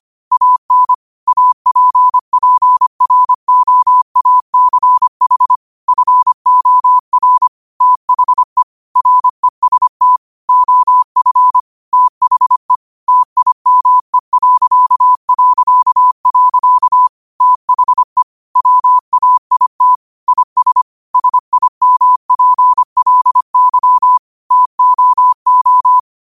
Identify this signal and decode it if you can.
Morse code, heard as a keyed tone, about 25 wpm, 1000 Hz. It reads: AN APPROACH FOR THE REST OF THE TIME... THE WAIT IS SIMPLY TOO
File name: qotdfiles / 25wpm